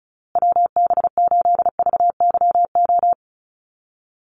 Morse code, heard as a keyed tone, about 34 wpm, 700 Hz. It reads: WB8VYO
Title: USA Call Signs Repeated 3x 35wpm